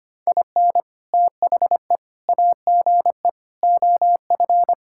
Morse code, heard as a keyed tone, about 25 wpm, 700 Hz. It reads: IN THE AGE OF
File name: practice / 25wpm / Sentences with 3-Word Phrases - 25wpm